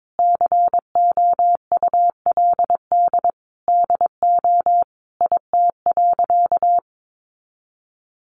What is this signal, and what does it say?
Morse code, heard as a keyed tone, about 22 wpm, 700 Hz. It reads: COULD DO IT.
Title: Sentences from Top 1100 Words 22wpm